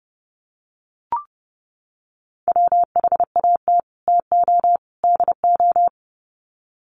Morse code, heard as a keyed tone, about 29 wpm, 700 Hz. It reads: WHAT TO DO
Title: Sets of 3 Words 30wpm